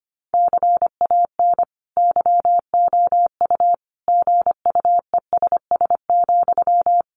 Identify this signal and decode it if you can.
Morse code, heard as a keyed tone, about 25 wpm, 700 Hz. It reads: CAN YOU GUESS,
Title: Sentences from Top 1500 Words 25wpm